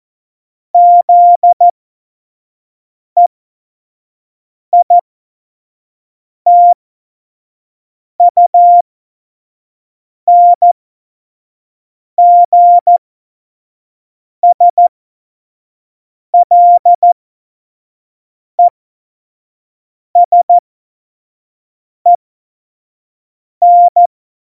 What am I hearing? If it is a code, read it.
Morse code, heard as a keyed tone, about 14 wpm, spaced as for 5 wpm, 700 Hz. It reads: ZEITUNGSLESEN
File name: Morsekurs / Witze / Witz 20. Oktober 2020, 25 F70